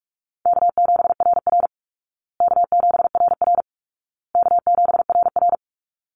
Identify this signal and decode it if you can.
Morse code, heard as a keyed tone, about 44 wpm, 700 Hz. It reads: K7RR K7RR K7RR
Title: Easy Call Signs Repeated 3x - 45wpm